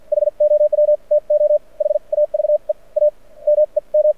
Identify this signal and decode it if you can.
Morse code, heard as a keyed tone, about 52 wpm, spaced as for 41 wpm, 590 Hz. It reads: HOW TO HAVE A MEM
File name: archived / CRSnet